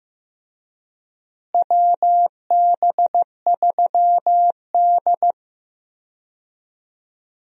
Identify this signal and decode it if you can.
Morse code, heard as a keyed tone, about 15 wpm, 700 Hz. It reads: WB3D
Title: USA Call Signs - 4 Characters - Repeated 3x - 15wpm